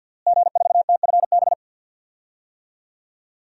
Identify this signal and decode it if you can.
Morse code, heard as a keyed tone, about 51 wpm, 700 Hz. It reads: G4TFB
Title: DXCC-100 Call Signs - 5 Characters - Repeated 3x - 50wpm